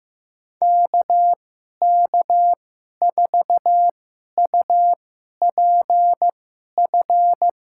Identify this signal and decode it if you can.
Morse code, heard as a keyed tone, about 16 wpm, spaced as for 11 wpm, 700 Hz. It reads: KK4UPF
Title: USA Call Signs - 6 Characters - 2x Character Spacing - 15wpm